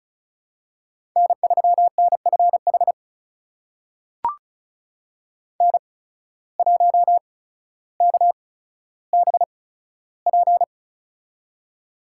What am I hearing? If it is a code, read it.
Morse code, heard as a keyed tone, about 34 wpm, spaced as for 15 wpm, 700 Hz. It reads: N3NFH N1KBP
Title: USA Call Signs - 5 Characters - 8x Character Spacing - 35wpm